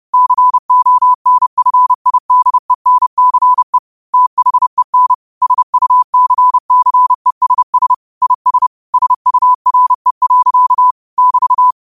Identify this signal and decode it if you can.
Morse code, heard as a keyed tone, about 30 wpm, 1000 Hz. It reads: CONFIDENCE THEN SUCCESS IS SURE. <BT>
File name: qotdfiles / 30wpm